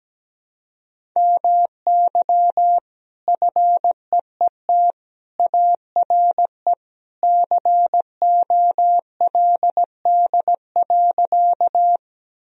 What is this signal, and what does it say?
Morse code, heard as a keyed tone, about 17 wpm, 700 Hz. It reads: MY FEET ARE COLD.
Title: Sentences from Top 1000 Words - Review Part 1 - 17wpm